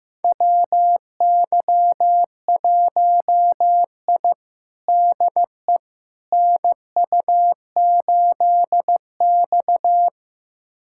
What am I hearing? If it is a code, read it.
Morse code, heard as a keyed tone, about 16 wpm, 700 Hz. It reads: WY1I DE NU8X
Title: USA Callsign DE Callsign - 4 Characters - 15wpm